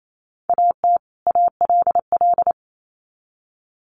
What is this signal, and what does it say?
Morse code, heard as a keyed tone, about 28 wpm, 700 Hz. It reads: AT ALL
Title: Sets of 2 Words 28wpm